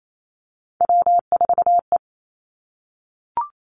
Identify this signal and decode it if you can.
Morse code, heard as a keyed tone, about 28 wpm, 700 Hz. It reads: W4E
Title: Three Challenging Letter-Number Combinations 28wpm